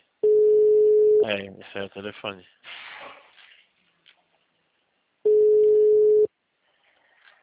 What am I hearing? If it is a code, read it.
Ringback tone (Europe, 425 Hz), with speech.
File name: out- -2020- - - .14